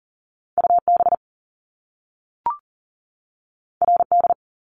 Morse code, heard as a keyed tone, about 42 wpm, 700 Hz. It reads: UB RD
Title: Top Two Letter Combinations in Words 40wpm